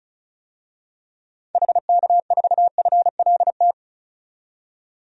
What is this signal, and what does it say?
Morse code, heard as a keyed tone, about 34 wpm, 700 Hz. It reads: HK4FLT
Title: DXCC-100 Call Signs - 6 Characters - 35wpm